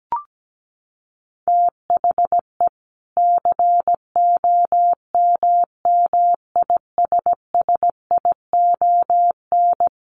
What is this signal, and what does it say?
Morse code, heard as a keyed tone, about 17 wpm, 700 Hz. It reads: THE COMMISSION
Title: Sentences from Top 1400 Words 17wpm